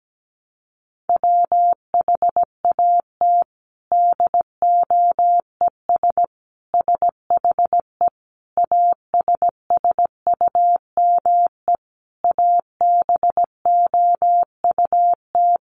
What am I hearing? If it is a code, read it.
Morse code, heard as a keyed tone, about 17 wpm, 700 Hz. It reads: WHAT DOES SHE ASSUME ABOUT
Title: Sentences from Top 1500 Words 17wpm